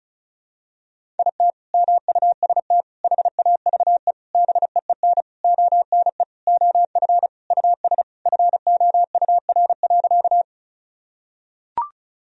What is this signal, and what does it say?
Morse code, heard as a keyed tone, about 36 wpm, 700 Hz. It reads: IT MUST HAVE BEEN ONE OF US FOUR.
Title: Sentences from Top 1000 Words - Review Part 1 - 35wpm